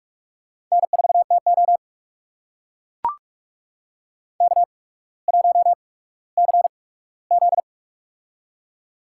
Morse code, heard as a keyed tone, about 44 wpm, spaced as for 19 wpm, 690 Hz. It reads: N4TO K1CZ
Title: USA Call Signs - 4 Characters - 8x Character Spacing - 45wpm